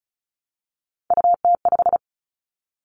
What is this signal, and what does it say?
Morse code, heard as a keyed tone, about 34 wpm, 700 Hz. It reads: UT5
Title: Callsign Prefixes for Top 25 DXCC - Repeated 3x - 35wpm